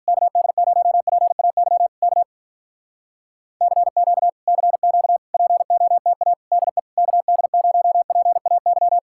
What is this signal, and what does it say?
Morse code, heard as a keyed tone, about 46 wpm, 700 Hz. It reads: KD0PAY K CQ CQ POTA DE KD0PAY